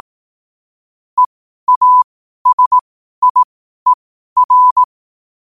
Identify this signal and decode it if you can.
Morse code, heard as a keyed tone, about 18 wpm, spaced as for 12 wpm, 1000 Hz. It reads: EASIER
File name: qotdfiles / 12wpm